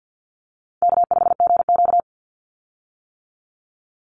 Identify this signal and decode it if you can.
Morse code, heard as a keyed tone, about 50 wpm, 700 Hz. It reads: K5GQ
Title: Call Signs - 50wpm